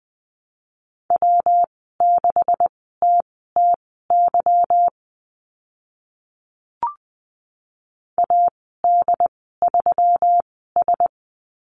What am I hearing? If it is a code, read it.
Morse code, heard as a keyed tone, about 20 wpm, spaced as for 14 wpm, 700 Hz. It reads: W6TTY AD3S
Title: 147 Lesson 24g - Letter y - Call Signs 20wpm